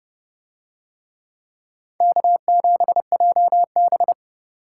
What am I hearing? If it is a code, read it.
Morse code, heard as a keyed tone, about 29 wpm, 700 Hz. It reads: K7JB